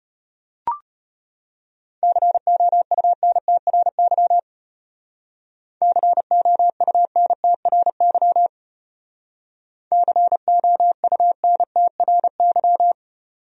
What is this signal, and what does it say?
Morse code, heard as a keyed tone, about 30 wpm, 700 Hz. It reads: COUNTRY COUNTRY COUNTRY
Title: Speed-Racing Top 300 Words 25wpm